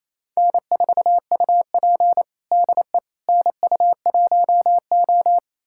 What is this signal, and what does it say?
Morse code, heard as a keyed tone, about 28 wpm, 700 Hz. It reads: N4UP DE NU1O
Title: USA Callsign DE Callsign - 4 Characters - 28wpm